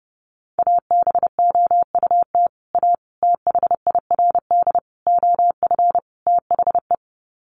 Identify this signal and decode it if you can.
Morse code, heard as a keyed tone, about 30 wpm, 700 Hz. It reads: ABOUT A THIRD OF THE
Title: Sets of 5 Words Encore2 30wpm